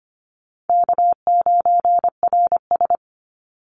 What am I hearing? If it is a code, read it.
Morse code, heard as a keyed tone, about 25 wpm, 700 Hz. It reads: K9RS
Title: Call Signs - 25wpm